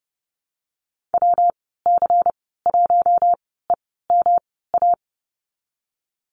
Morse code, heard as a keyed tone, about 30 wpm, spaced as for 17 wpm, 700 Hz. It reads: WC1EMA
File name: Lesson 19g - Letter w - Call Signs 30wpm